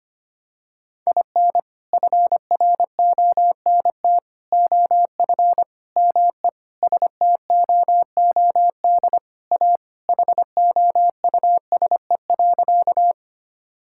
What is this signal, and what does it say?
Morse code, heard as a keyed tone, about 25 wpm, 700 Hz. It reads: IN FRONT OF ME STOOD A HOUSE.
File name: practice / 25wpm / Sentences with 4-Word Phrases - 25wpm